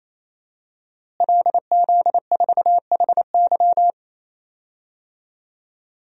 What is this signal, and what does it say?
Morse code, heard as a keyed tone, about 28 wpm, 700 Hz. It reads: LZ4HY
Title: DXCC-100 Call Signs - 5 Characters - Repeated 3x - 28wpm